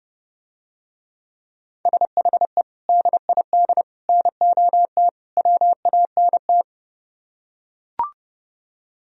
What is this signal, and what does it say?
Morse code, heard as a keyed tone, about 30 wpm, 700 Hz. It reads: SHE DID NOT WANT